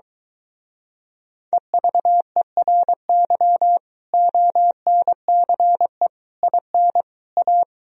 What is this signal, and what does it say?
Morse code, heard as a keyed tone, about 23 wpm, 700 Hz. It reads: EVERY ONCE IN A